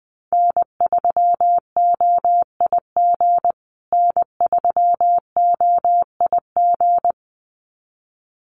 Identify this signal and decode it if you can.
Morse code, heard as a keyed tone, about 20 wpm, 700 Hz. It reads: N3OIG N3OIG
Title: USA Call Signs Repeated 2x 20wpm